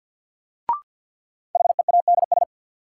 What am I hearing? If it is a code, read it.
Morse code, heard as a keyed tone, about 52 wpm, 700 Hz. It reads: HEADS
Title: Sentences from Top 1000 Words - Review Part 3 - New Word Drill - 50wpm